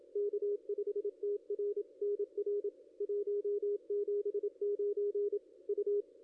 Morse code, heard as a keyed tone, about 27 wpm, 410 Hz, with radio noise. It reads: K5TR NR 179 U
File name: contests / ss-cw-k5tr-op-n5rz -0600